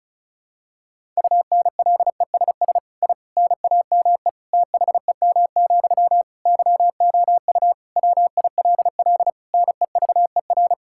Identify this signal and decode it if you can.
Morse code, heard as a keyed tone, about 36 wpm, 700 Hz. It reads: UNLESS I NAME THEM, YOU WILL NEVER
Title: Sentences from Top 1000 Words - Review Part 6 - 35wpm